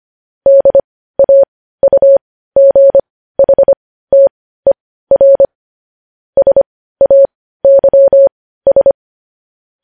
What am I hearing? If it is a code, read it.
Morse code, heard as a keyed tone, about 25 wpm, spaced as for 15 wpm, 550 Hz. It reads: DAUGHTER SAYS
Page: news headlines in morse code at 15 wpm